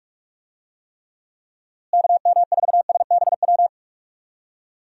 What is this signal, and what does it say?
Morse code, heard as a keyed tone, about 46 wpm, 700 Hz. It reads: KM4SBW
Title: USA Call Signs - 6 Characters - Repeated 3x - 45wpm